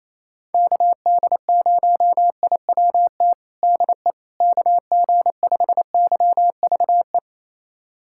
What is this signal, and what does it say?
Morse code, heard as a keyed tone, about 28 wpm, 700 Hz. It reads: KD0IWT DE KG5YVE
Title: USA Callsign DE Callsign - 6 Characters - 28wpm